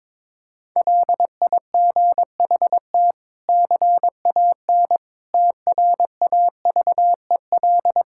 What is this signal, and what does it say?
Morse code, heard as a keyed tone, about 22 wpm, 700 Hz. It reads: LIGHT CAN TRAVEL
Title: Sentences from Top 1000 Words - Review Part 6 - 22wpm